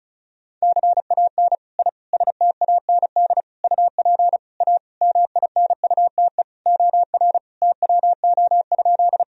Morse code, heard as a keyed tone, about 36 wpm, 700 Hz. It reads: CAN I STAND UP A MINUTE OR TWO?
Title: Sentences from Top 1000 Words - Review Part 1 - 35wpm